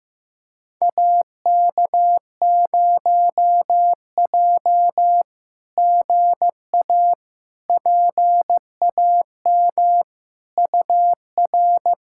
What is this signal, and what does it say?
Morse code, heard as a keyed tone, about 15 wpm, 700 Hz. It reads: AK0J GA PAM UR
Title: SOTA QSO Practice 15wpm